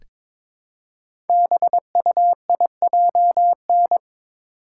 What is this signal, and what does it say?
Morse code, heard as a keyed tone, about 22 wpm, 700 Hz. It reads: BUIJN
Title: Code Groups - 5 Letters - 22wpm